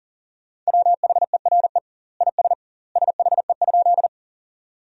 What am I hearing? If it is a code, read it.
Morse code, heard as a keyed tone, about 40 wpm, 700 Hz. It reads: WHERE IS SHE?
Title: 2x Word-Spacing - Sentences from Top 200 Words - 40wpm